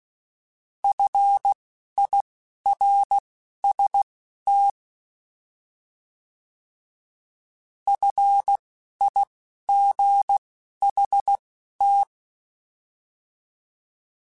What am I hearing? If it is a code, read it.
Morse code, heard as a keyed tone, about 16 wpm, spaced as for 11 wpm, 800 Hz. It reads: FIRST FIGHT
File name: Alphabetical / Alphabetical mp3s / Fs @10